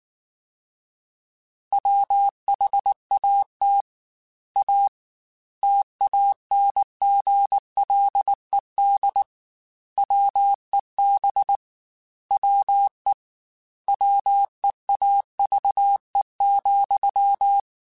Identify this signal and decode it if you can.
Morse code, heard as a keyed tone, about 19 wpm, 800 Hz. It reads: WHAT A TANGLED WEB WE WEAVE,
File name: Quotes / Quotes-mp3s / Deceive @18